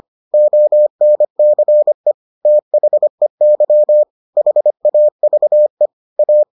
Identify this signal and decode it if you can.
Morse code, heard as a keyed tone, about 25 wpm, 600 Hz. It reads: ONCE THEY HAVE A